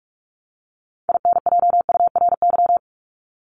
Morse code, heard as a keyed tone, about 44 wpm, 700 Hz. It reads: INJURY
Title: Top 1751-2000 Words 45wpm